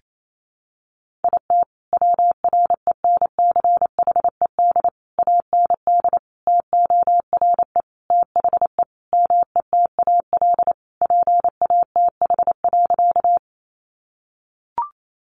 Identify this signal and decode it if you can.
Morse code, heard as a keyed tone, about 28 wpm, 700 Hz. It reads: IT WRENCHED AND TORE THE METAL PATH.
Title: A Sound Of Thunder 28wpm